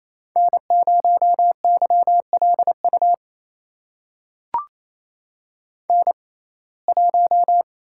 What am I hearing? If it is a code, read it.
Morse code, heard as a keyed tone, about 28 wpm, spaced as for 20 wpm, 700 Hz. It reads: N0YLU N1
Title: USA Call Signs - 5 Characters - 6x Character Spacing - 28wpm